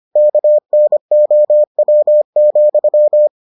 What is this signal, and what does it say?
Morse code, heard as a keyed tone, about 25 wpm, 600 Hz. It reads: KNOW,